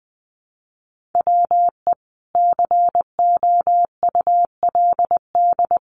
Morse code, heard as a keyed tone, about 20 wpm, 700 Hz. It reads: WE COULD
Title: Sentences from Top 1000 Words 20wpm